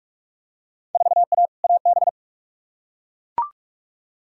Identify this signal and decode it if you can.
Morse code, heard as a keyed tone, about 43 wpm, 700 Hz. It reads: 4A AB
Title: Field Day Exchanges Encore - 45wpm